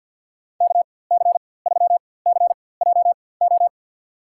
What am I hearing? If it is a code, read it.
Morse code, heard as a keyed tone, about 49 wpm, spaced as for 23 wpm, 700 Hz. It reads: KC3CJO
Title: USA Call Signs - 6 Characters - 4x Character Spacing - 50wpm